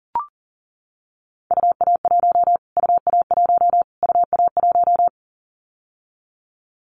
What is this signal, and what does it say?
Morse code, heard as a keyed tone, about 40 wpm, 700 Hz. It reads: UA1 UA1 UA1
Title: Callsign Prefixes for Top 25 DXCC - Repeated 3x - 40wpm